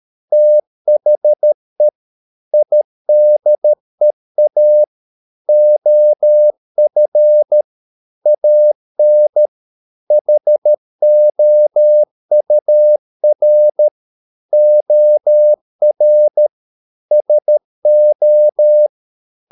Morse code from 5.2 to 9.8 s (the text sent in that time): OF AN